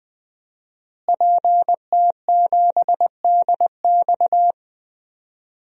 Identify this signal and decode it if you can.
Morse code, heard as a keyed tone, about 20 wpm, 700 Hz. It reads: PT7DX
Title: DXCC-100 Call Signs - 5 Characters - Repeated 3x - 20wpm